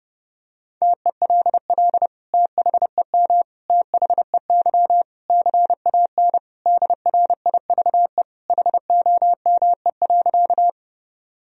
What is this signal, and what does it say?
Morse code, heard as a keyed tone, about 30 wpm, 700 Hz. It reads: TELL THEM THEY CAN DRIVE HOME.
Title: Sentences from Top 1000 Words - Review Part 2 - 30wpm